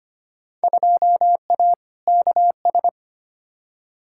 Morse code, heard as a keyed tone, about 25 wpm, 700 Hz. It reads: 2A KS
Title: Field Day Exchanges - 25wpm